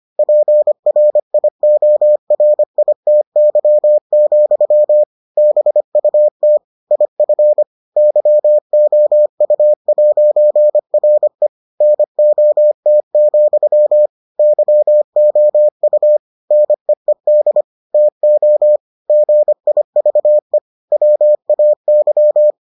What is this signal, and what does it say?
Morse code, heard as a keyed tone, about 25 wpm, 600 Hz. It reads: PRIORITY, BUT IF YOU'RE NOT, YOU NEED TO GIVE WAY